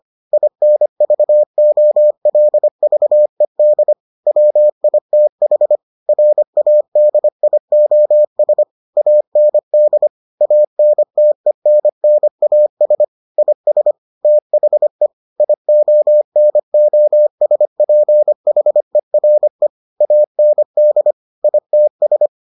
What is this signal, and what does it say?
Morse code, heard as a keyed tone, about 25 wpm, 600 Hz. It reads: INVOLVED WITH RADIOS AND ANTENNAS IS THE IONOSPHERE AND ITS